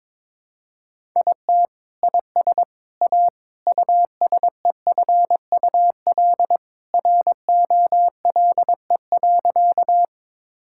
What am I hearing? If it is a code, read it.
Morse code, heard as a keyed tone, about 22 wpm, 700 Hz. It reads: IT IS A USEFUL ROLE.
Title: Sentences from Top 1000 Words - Review Part 6 - 22wpm